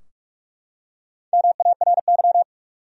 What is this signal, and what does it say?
Morse code, heard as a keyed tone, about 44 wpm, 700 Hz. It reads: MARY
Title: USA Common Names - Max 5 Letters - 45wpm